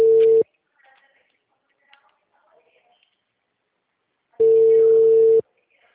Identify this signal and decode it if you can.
Ringback tone (China, 450 Hz), with speech.